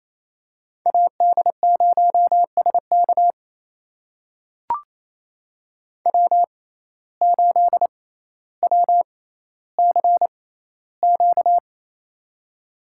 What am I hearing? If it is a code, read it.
Morse code, heard as a keyed tone, about 28 wpm, spaced as for 14 wpm, 700 Hz. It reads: AD0SK W8WCQ
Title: USA Call Signs - 5 Characters - 6x Character Spacing - 28wpm